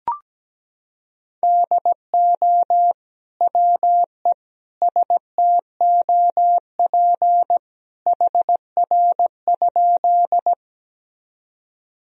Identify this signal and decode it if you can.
Morse code, heard as a keyed tone, about 17 wpm, 700 Hz. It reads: DO WE STOP HR?
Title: Sentences from Top 300 Words and Abbreviations 17wpm